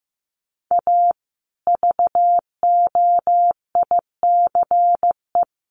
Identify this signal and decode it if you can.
Morse code, heard as a keyed tone, about 15 wpm, 700 Hz. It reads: A VOICE
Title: Sentences from Top 1200 Words 15wpm